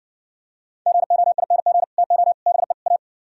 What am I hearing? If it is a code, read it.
Morse code, heard as a keyed tone, about 69 wpm, 700 Hz. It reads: GOING TO BE A